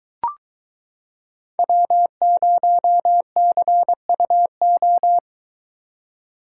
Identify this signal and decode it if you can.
Morse code, heard as a keyed tone, about 23 wpm, 700 Hz. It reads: W0CUO